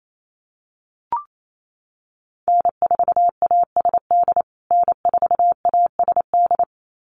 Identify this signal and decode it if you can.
Morse code, heard as a keyed tone, about 28 wpm, 700 Hz. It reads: N4ASD N4ASD
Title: USA Call Signs Repeated 2x 28wpm